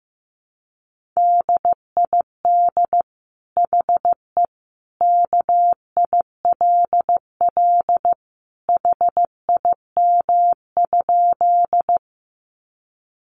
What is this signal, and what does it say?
Morse code, heard as a keyed tone, about 15 wpm, 700 Hz. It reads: DID HE KILL HIM?
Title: Sentences from Top 1400 Words 15wpm